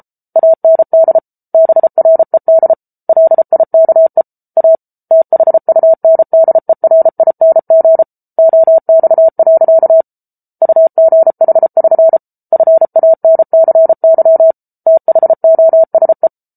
Morse code, heard as a keyed tone, about 34 wpm, 650 Hz. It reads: AND BRED LIKE A THUNDERING OX. UGH<SN> FANCY THOSE